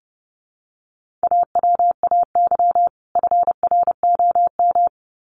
Morse code, heard as a keyed tone, about 30 wpm, 700 Hz. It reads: AWAY FROM
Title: Sets of 2 Words 30wpm